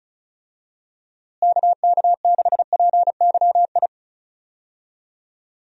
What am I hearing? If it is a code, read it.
Morse code, heard as a keyed tone, about 34 wpm, 700 Hz. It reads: KK6PYI
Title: USA Call Signs - 6 Characters - Repeated 3x - 35wpm